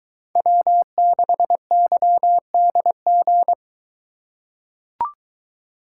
Morse code, heard as a keyed tone, about 22 wpm, 700 Hz. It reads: W6YDG